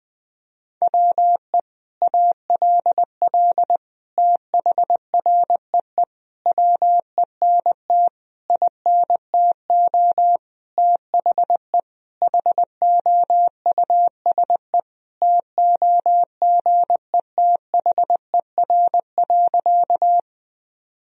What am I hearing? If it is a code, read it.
Morse code, heard as a keyed tone, about 20 wpm, 700 Hz. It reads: WE ALL THREE WENT INTO THE HOUSE TOGETHER.
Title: Sentences from Top 1000 Words - Review Part 1 - 20wpm